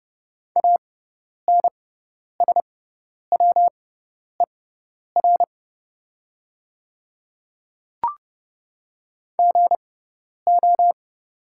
Morse code, heard as a keyed tone, about 30 wpm, spaced as for 10 wpm, 700 Hz. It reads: ANSWER GO
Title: 6x Character Spacing - Top 300 Words - 30wpm